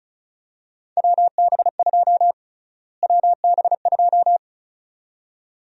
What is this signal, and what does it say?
Morse code, heard as a keyed tone, about 36 wpm, 700 Hz. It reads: WB2 WB2
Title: USA Call Sign Prefixes - Repeated 2x - 35wpm